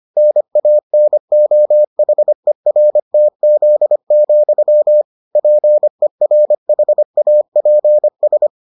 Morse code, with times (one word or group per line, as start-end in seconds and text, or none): 0.2-5.0 NANOHERTZ,
5.4-8.5 PERHAPS